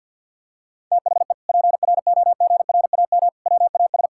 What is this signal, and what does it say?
Morse code, heard as a keyed tone, about 50 wpm, 700 Hz. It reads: THE PROGRAM WAS
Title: Sentences from Top 1000 Words - Review Part 5 - 50wpm